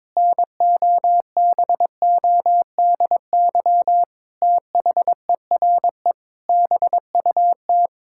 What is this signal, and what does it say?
Morse code, heard as a keyed tone, about 22 wpm, 700 Hz. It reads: NOBODY THERE BUT
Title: Sentences from Top 1600 Words 22wpm